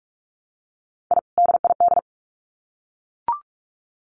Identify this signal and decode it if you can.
Morse code, heard as a keyed tone, about 44 wpm, 700 Hz. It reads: I DID
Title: Sets of 2 Words 45wpm